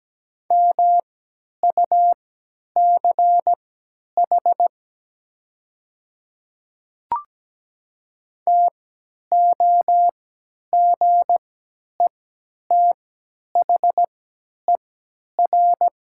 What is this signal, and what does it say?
Morse code, heard as a keyed tone, about 17 wpm, spaced as for 10 wpm, 700 Hz. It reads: MUCH TOGETHER
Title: 3x Character Spacing - Top 500 Words - 17wpm